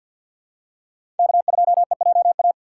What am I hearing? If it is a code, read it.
Morse code, heard as a keyed tone, about 53 wpm, 700 Hz. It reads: K2EJA
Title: USA Call Signs - 5 Characters - 3x Character Spacing - 50wpm